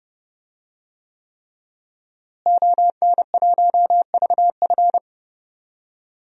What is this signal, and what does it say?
Morse code, heard as a keyed tone, about 30 wpm, 700 Hz. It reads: ON1VF
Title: DXCC-100 Call Signs - 5 Characters - Repeated 3x - 30wpm